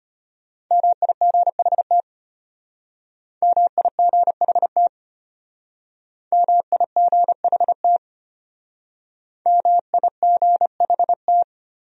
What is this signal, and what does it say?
Morse code, heard as a keyed tone, about 31 wpm, 700 Hz. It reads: MIGHT MIGHT MIGHT MIGHT